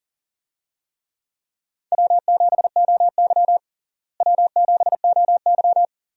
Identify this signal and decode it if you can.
Morse code, heard as a keyed tone, about 40 wpm, 700 Hz. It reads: W7OY W7OY
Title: USA Call Signs - 4 Characters - Repeated 3x - 40wpm